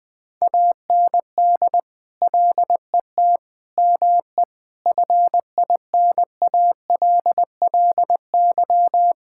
Morse code, heard as a keyed tone, about 20 wpm, 700 Hz. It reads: AND LET ME FINALLY
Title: Sentences from Top 1600 Words 20wpm